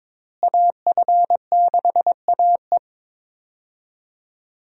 Morse code, heard as a keyed tone, about 22 wpm, 700 Hz. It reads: AF6AE
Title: USA Call Signs - 5 Characters - Repeated 3x - 22wpm